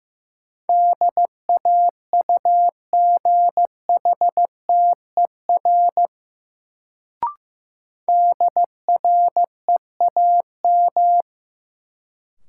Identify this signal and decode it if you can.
Morse code, heard as a keyed tone, about 15 wpm, 700 Hz. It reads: DAUGHTER DREAM
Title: Sentences from Top 1500 Words - New Word Drill Alt - 15wpm